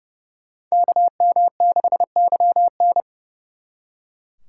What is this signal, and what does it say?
Morse code, heard as a keyed tone, about 30 wpm, 700 Hz. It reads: KM6YN
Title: Speed Racing - USA 2x2 Call Signs 30wpm